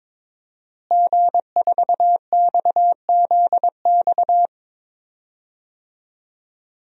Morse code, heard as a keyed tone, about 21 wpm, 700 Hz. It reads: G4XZX
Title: DXCC-100 Call Signs - 5 Characters - Repeated 3x - 22wpm